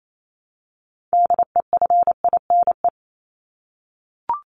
Morse code, heard as a keyed tone, about 28 wpm, 700 Hz. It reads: DEFINE
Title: ED Suffix - Top 100 Words with Base Words 28wpm